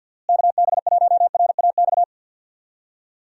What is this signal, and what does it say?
Morse code, heard as a keyed tone, about 48 wpm, 700 Hz. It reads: KB1RAX
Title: USA Call Signs - 6 Characters - Repeated 2x - 50wpm